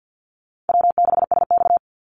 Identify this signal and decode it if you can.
Morse code, heard as a keyed tone, about 50 wpm, 700 Hz. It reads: W6SX
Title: Easy Call Signs Repeated 3x - 50wpm